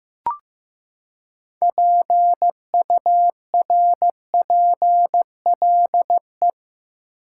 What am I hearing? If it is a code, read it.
Morse code, heard as a keyed tone, about 15 wpm, 700 Hz. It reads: PURPLE
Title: Six Letter Words - 15wpm